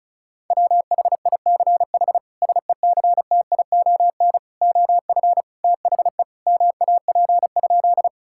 Morse code, heard as a keyed tone, about 36 wpm, 700 Hz. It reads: WHICH SECTION OF THE MAP?
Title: Sentences from Top 1000 Words - Review Part 5 - 35wpm